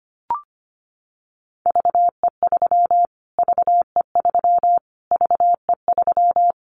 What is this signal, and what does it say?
Morse code, heard as a keyed tone, about 25 wpm, 700 Hz. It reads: VE3 VE3 VE3
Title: Callsign Prefixes for Top 25 DXCC - Repeated 3x - 25wpm